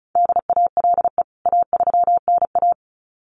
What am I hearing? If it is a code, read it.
Morse code, heard as a keyed tone, about 34 wpm, 700 Hz. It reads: DALE A3NA
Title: CWT Exchange 35wpm